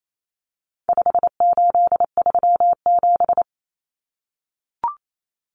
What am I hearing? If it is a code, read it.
Morse code, heard as a keyed tone, about 28 wpm, 700 Hz. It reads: 5837
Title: Numbers - 4-Digit 28wpm